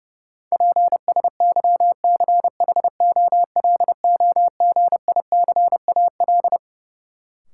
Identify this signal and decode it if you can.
Morse code, heard as a keyed tone, about 30 wpm, 700 Hz. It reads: PSYCHOLOGICAL
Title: Sentences from Top 1400 Words - New Word Drill - 30wpm